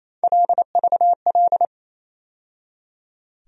Morse code, heard as a keyed tone, about 27 wpm, 700 Hz. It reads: LVL
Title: USA Call Sign Postfixes - 3 Letters 28wpm